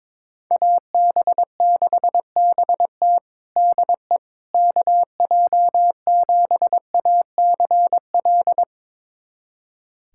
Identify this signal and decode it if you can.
Morse code, heard as a keyed tone, about 22 wpm, 700 Hz. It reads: AB6BT DE KJ7ACL
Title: USA Callsign DE Callsign 22wpm